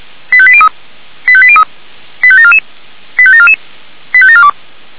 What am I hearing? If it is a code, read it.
Five-tone selective call (ZVEI), heard as quick runs of five tones, tones of about 70 ms. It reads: 85803 85803 85744 85744 85742